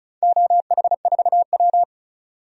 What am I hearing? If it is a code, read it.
Morse code, heard as a keyed tone, about 36 wpm, 700 Hz. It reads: OH4W
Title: DXCC-100 Call Signs - 4 Characters - 35wpm